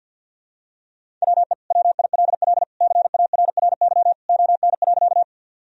Morse code, heard as a keyed tone, about 50 wpm, 700 Hz. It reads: WE WILL CARRY ON.
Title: Sentences from Top 1000 Words - Review Part 1 - 50wpm